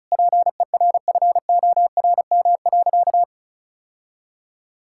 Morse code, heard as a keyed tone, about 36 wpm, 700 Hz. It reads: PERFORM.
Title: Sentences from Top 1000 Words - Review Part 5 - 35wpm